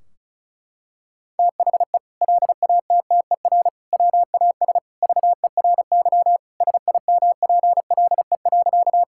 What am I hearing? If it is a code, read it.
Morse code, heard as a keyed tone, about 36 wpm, 700 Hz. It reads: THE LATTER WAS VERY SIMPLE.
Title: Sentences from Top 1600 Words 35wpm